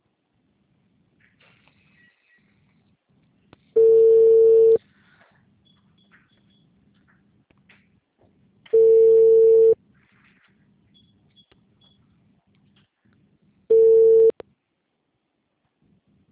Ringback tone (China, 450 Hz).
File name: out- -2009- - - .4423